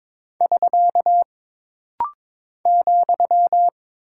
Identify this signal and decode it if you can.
Morse code, heard as a keyed tone, about 22 wpm, 700 Hz. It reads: <SK> ,
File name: practice / 22wpm / Single Punctuation-Prosign - Rapid-Fire - 22wpm